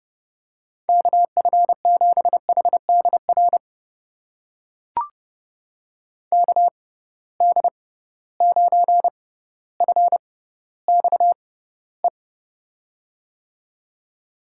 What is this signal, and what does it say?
Morse code, heard as a keyed tone, about 31 wpm, spaced as for 16 wpm, 700 Hz. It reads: KF7HDR KD9FXE
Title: USA Call Signs - 6 Characters - 6x Character Spacing - 30wpm